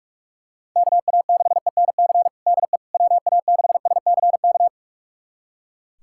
Morse code, heard as a keyed tone, about 41 wpm, 700 Hz. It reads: KA6ENC DE WA6SCK
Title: USA Callsign DE Callsign - 6 Characters - 45wpm